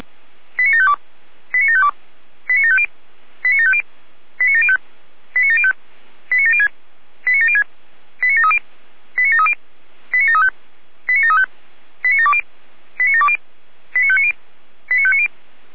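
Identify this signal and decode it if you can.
Five-tone selective call (ZVEI), heard as quick runs of five tones, tones of about 70 ms. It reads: 89752 89752 89750 89750 89705 89705 89706 89706 89730 89730 89735 89735 89720 89720 89590 89590
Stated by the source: Five-tone ZVEI selective call 89531 received by SDR.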